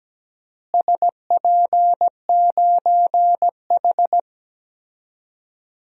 Morse code, heard as a keyed tone, about 17 wpm, 700 Hz. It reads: SP9H
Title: DXCC-100 Call Signs - 4 Characters - Repeated 3x - 17wpm